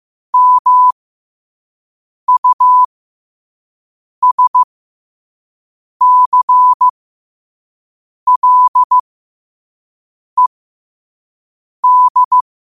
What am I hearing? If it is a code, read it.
Morse code, heard as a keyed tone, about 15 wpm, spaced as for 5 wpm, 1000 Hz. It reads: MUSCLED